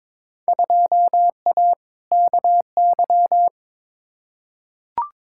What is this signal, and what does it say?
Morse code, heard as a keyed tone, about 22 wpm, 700 Hz. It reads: 2A KY
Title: Field Day Exchanges - 22wpm